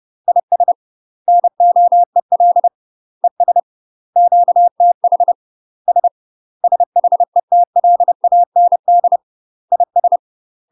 Morse code, heard as a keyed tone, about 30 wpm, 700 Hz. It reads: IS NOEL ES QTH S SHETLAND IS